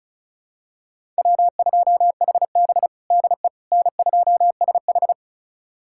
Morse code, heard as a keyed tone, about 36 wpm, 700 Hz. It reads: W2HB DE N2SH
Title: USA Callsign DE Callsign - 4 Characters - 35wpm